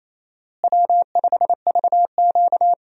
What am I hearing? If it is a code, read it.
Morse code, heard as a keyed tone, about 28 wpm, 700 Hz. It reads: W5VQ
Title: USA Call Signs - 4 Characters - Repeated 3x - 28wpm